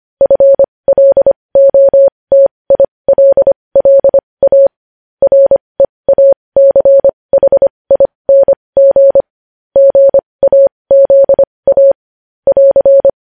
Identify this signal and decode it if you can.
Morse code, heard as a keyed tone, about 25 wpm, spaced as for 20 wpm, 550 Hz. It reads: FLOTILLA REACHING GAZA <AR>
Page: News Headlines in Morse Code at 20 WPM